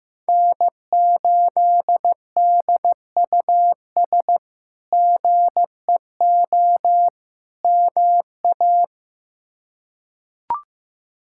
Morse code, heard as a keyed tone, about 15 wpm, 700 Hz. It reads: N8DUS GEO MA
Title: K1USN SST QSO Practice - 15wpm